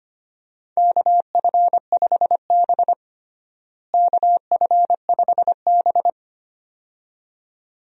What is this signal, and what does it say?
Morse code, heard as a keyed tone, about 25 wpm, 700 Hz. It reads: KF5B KF5B
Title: USA Call Signs - 4 Characters - Repeated 2x - 25wpm